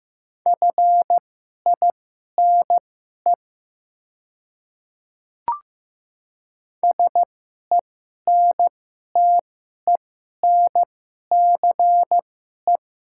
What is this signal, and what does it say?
Morse code, heard as a keyed tone, about 15 wpm, spaced as for 11 wpm, 700 Hz. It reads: FINE SENTENCE
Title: 2x Character Spacing - Top 500 Words - 15wpm